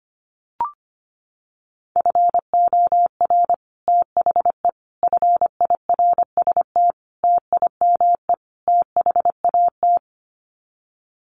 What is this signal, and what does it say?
Morse code, heard as a keyed tone, about 25 wpm, 700 Hz. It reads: FOR THE FIRST TIME THAT
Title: Sets of 5 Words Encore 25wpm